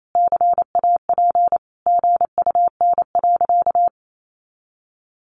Morse code, heard as a keyed tone, about 28 wpm, 700 Hz. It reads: CAP GUN.
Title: A Sound Of Thunder 28wpm